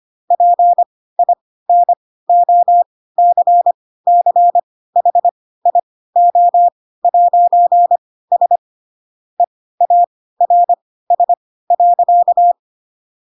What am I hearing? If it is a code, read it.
Morse code, heard as a keyed tone, about 25 wpm, spaced as for 16 wpm, 700 Hz. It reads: PINOCCHIO'S EARS.
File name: Pinocchio / Pinocchio ch3 10 25 ik1hge